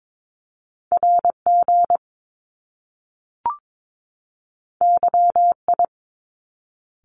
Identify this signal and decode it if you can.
Morse code, heard as a keyed tone, about 22 wpm, 700 Hz. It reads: RG YI
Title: Top Two Letter Combinations in Words 22wpm